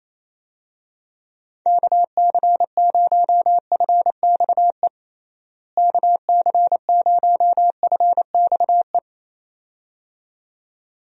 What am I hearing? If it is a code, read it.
Morse code, heard as a keyed tone, about 27 wpm, 700 Hz. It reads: KC0FXE KC0FXE
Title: USA Call Signs - 6 Characters - Repeated 2x - 28wpm